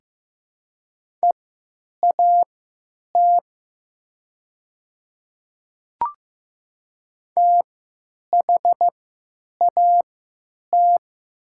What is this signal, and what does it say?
Morse code, heard as a keyed tone, about 15 wpm, spaced as for 9 wpm, 700 Hz. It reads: EAT THAT
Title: 3x Character Spacing - Top 300 Words - 15wpm